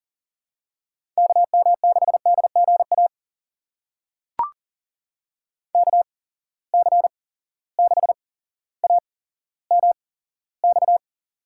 Morse code, heard as a keyed tone, about 40 wpm, spaced as for 17 wpm, 700 Hz. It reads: KM6DGA KC6AMX
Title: USA Call Signs - 6 Characters - 8x Character Spacing - 40wpm